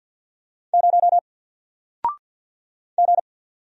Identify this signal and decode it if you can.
Morse code, heard as a keyed tone, about 48 wpm, 700 Hz. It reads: 0 G
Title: Single Character (Letter:Number:Punctuation:Prosign) 50wpm